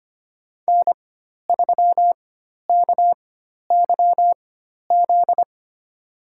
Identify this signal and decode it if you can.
Morse code, heard as a keyed tone, about 25 wpm, spaced as for 12 wpm, 700 Hz. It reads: N3KYZ
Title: USA Call Signs - 5 Characters - 4x Character Spacing - 25wpm